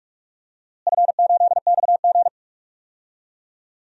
Morse code, heard as a keyed tone, about 46 wpm, 700 Hz. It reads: F8XG
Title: DXCC-100 Call Signs - 4 Characters - Repeated 2x - 45wpm